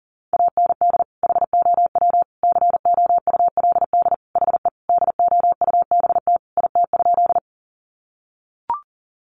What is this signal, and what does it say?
Morse code, heard as a keyed tone, about 40 wpm, 700 Hz. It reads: AND HOW COULD HE DOUBT IT?
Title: Sentences from Top 1500 Words 40wpm